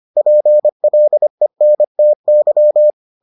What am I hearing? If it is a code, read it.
Morse code, heard as a keyed tone, about 25 wpm, 600 Hz. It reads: PLENTY